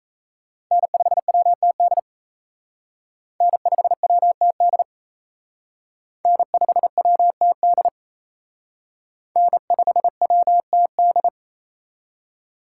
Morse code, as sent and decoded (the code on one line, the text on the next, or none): -. ..... .-- - -.. / -. ..... .-- - -.. / -. ..... .-- - -.. / -. ..... .-- - -..
N5WTD N5WTD N5WTD N5WTD